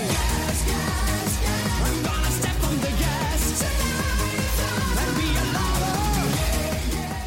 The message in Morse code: none